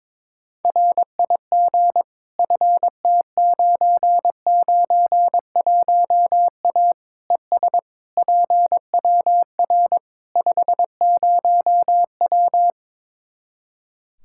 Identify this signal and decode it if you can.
Morse code, heard as a keyed tone, about 22 wpm, 700 Hz. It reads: RIG FT991A ES PWR 50W
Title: Practice QSO - Exchange 2 - Info - 22wpm